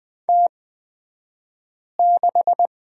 Morse code, heard as a keyed tone, about 20 wpm, spaced as for 5 wpm, 700 Hz. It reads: T6
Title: 242 Lesson 39d - Letter z - Two Characters 20wpm